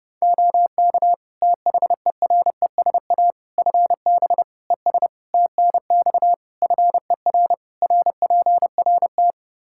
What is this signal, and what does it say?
Morse code, heard as a keyed tone, about 30 wpm, 700 Hz. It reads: OK THERESA FB ES TNX FER RPRT
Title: Practice QSO - Exchange 2 - Info - 30wpm